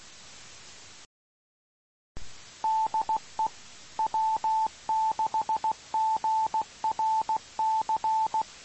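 Morse code, heard as a keyed tone, about 16 wpm, 880 Hz. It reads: DE W6GRC
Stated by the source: Morse code recorded on 2 m at 147.105 MHz.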